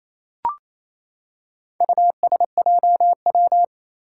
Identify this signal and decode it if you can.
Morse code, heard as a keyed tone, about 28 wpm, 700 Hz. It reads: USJW
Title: Four Challenging Letter-Number Combinations 28wpm